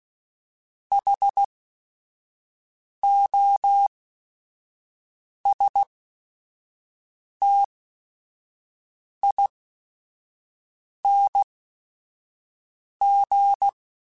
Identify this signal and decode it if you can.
Morse code, heard as a keyed tone, about 16 wpm, spaced as for 5 wpm, 800 Hz. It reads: HOSTING
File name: Special INGs / Special INGs-mp3s / Sngl Ing @5